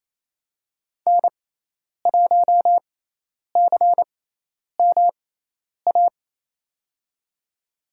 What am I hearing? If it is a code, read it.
Morse code, heard as a keyed tone, about 27 wpm, spaced as for 10 wpm, 700 Hz. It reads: N1CMA
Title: USA Call Signs - 5 Characters - 6x Character Spacing - 28wpm